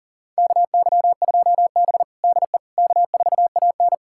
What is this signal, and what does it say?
Morse code, heard as a keyed tone, about 40 wpm, 700 Hz. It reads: KY2B DE K4AN
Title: USA Callsign DE Callsign - 4 Characters - 40wpm